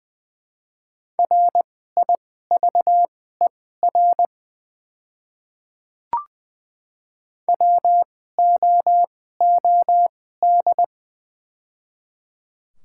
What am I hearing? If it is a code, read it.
Morse code, heard as a keyed tone, about 20 wpm, spaced as for 15 wpm, 690 Hz. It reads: RIVER WOOD
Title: 2x Character Spacing - Top 500 Words - 20wpm